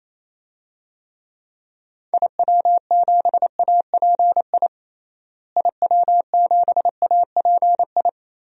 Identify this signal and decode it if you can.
Morse code, heard as a keyed tone, about 28 wpm, 700 Hz. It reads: IW7API IW7API
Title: DXCC-100 Call Signs - 6 Characters - Repeated 3x - 28wpm